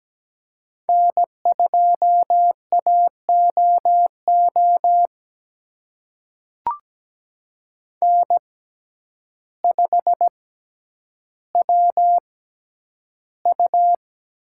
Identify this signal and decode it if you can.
Morse code, heard as a keyed tone, about 17 wpm, spaced as for 9 wpm, 700 Hz. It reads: N2AOO N5WU
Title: USA Call Signs - 6x Character Spacing - 17wpm